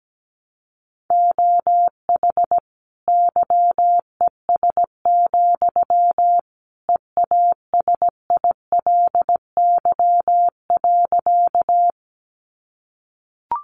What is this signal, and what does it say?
Morse code, heard as a keyed tone, about 17 wpm, 700 Hz. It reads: OH YES, EASILY.
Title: Sentences from Top 900 Words 17wpm